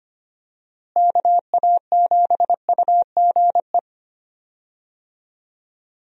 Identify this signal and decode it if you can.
Morse code, heard as a keyed tone, about 25 wpm, 700 Hz. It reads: KA7UGE
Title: USA Call Signs - 6 Characters - Repeated 3x - 25wpm